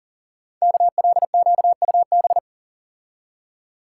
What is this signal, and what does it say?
Morse code, heard as a keyed tone, about 39 wpm, 690 Hz. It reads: KLQUB
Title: Code Groups - 5 Letter:Numbers - 40wpm